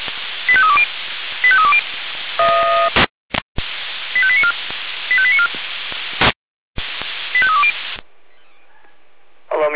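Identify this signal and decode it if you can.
Five-tone selective call (ZVEI), heard as quick runs of five tones, tones of about 70 ms. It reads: 85320 85320 85894 85894 85320
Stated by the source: ZVEI five-tone selective call, received on SDR.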